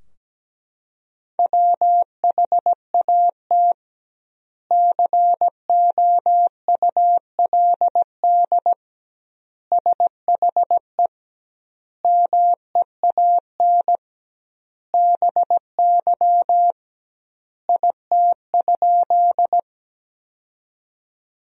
Morse code, heard as a keyed tone, about 17 wpm, 700 Hz. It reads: WHAT COULD SHE MEAN BY IT?
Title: 2x Word-Spacing - Sentences from Top 200 Words - 17wpm